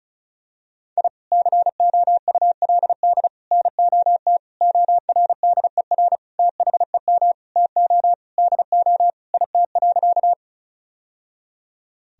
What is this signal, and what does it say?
Morse code, heard as a keyed tone, about 34 wpm, 700 Hz. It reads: I COULD NOT ORDER THEM TO DO IT.
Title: Sentences from Top 1000 Words - Review Part 2 - 35wpm